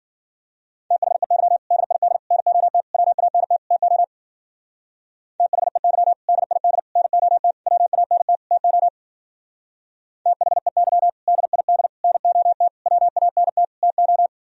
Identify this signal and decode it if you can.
Morse code, heard as a keyed tone, about 48 wpm, 700 Hz. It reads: THEY DID NOT WANT TO THEY DID NOT WANT TO THEY DID NOT WANT TO
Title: Speed-Racing - Sets of 5 Words 40wpm